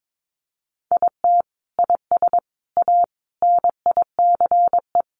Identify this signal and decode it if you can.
Morse code, heard as a keyed tone, about 22 wpm, 700 Hz. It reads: IT IS A NICE
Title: Sentences from Top 1100 Words 22wpm